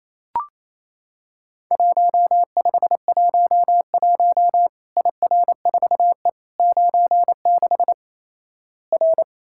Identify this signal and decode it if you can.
Morse code, heard as a keyed tone, about 28 wpm, 700 Hz. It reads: 1511 IR4E 96 R
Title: Worked All Europe DX Contest - QTC Exchange - 28wpm